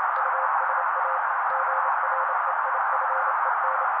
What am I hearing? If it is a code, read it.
Morse code, heard as a keyed tone, about 28 wpm, 530 Hz. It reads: USA GREIFEN
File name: daily / 27wpm noise